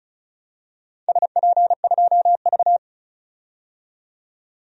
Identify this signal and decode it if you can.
Morse code, heard as a keyed tone, about 36 wpm, 690 Hz. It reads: SP2V
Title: DXCC-100 Call Signs - 4 Characters - 35wpm